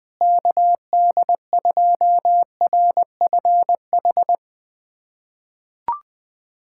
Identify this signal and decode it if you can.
Morse code, heard as a keyed tone, about 19 wpm, 700 Hz. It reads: KD2RFH
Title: USA Callsign DE Callsign 20wpm